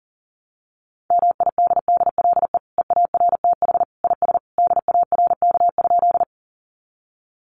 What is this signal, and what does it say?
Morse code, heard as a keyed tone, about 40 wpm, 700 Hz. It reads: MIDDLE EARTH IS DARK?
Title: Sentences from Top 900 Words 40wpm